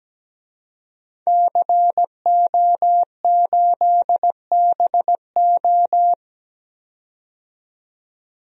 Morse code, heard as a keyed tone, about 17 wpm, 700 Hz. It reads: CO8BO
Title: DXCC-100 Call Signs - 5 Characters - 17wpm